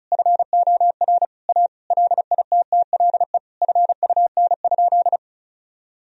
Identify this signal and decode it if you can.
Morse code, heard as a keyed tone, about 36 wpm, 700 Hz. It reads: FOR A LITTLE FUN?
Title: Sentences from Top 1000 Words - Review Part 1 - 35wpm